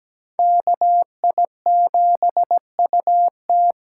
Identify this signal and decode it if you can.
Morse code, heard as a keyed tone, about 20 wpm, 700 Hz. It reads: KI7UT
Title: Speed Racing - USA Call Signs 17wpm